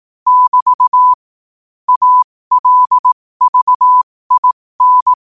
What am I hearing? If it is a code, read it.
Morse code, heard as a keyed tone, about 18 wpm, spaced as for 15 wpm, 1000 Hz. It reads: <BT> ALVIN
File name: qotdfiles / 15wpm